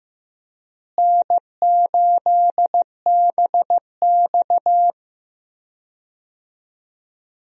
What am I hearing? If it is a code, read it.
Morse code, heard as a keyed tone, about 15 wpm, 700 Hz. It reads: N8BX
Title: USA Call Signs - 4 Characters - Repeated 3x - 15wpm